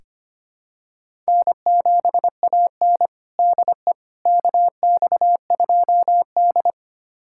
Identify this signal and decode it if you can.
Morse code, heard as a keyed tone, about 25 wpm, 700 Hz. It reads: N7AN DE KX2D
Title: USA Callsign DE Callsign - 4 Characters - 25wpm